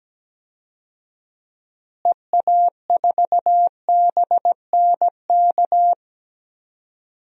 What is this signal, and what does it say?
Morse code, heard as a keyed tone, about 17 wpm, 700 Hz. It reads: EA4BNK
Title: DXCC-100 Call Signs - 6 Characters - Repeated 2x - 17wpm